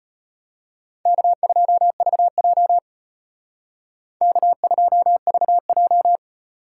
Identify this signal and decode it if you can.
Morse code, heard as a keyed tone, about 33 wpm, 700 Hz. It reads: K2VJ K2VJ